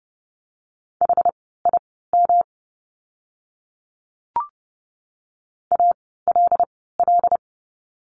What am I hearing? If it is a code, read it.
Morse code, heard as a keyed tone, about 30 wpm, spaced as for 17 wpm, 700 Hz. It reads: HIM ALL
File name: course / 30:10wpm / 237 Lesson 38f - Letter q - Words 30wpm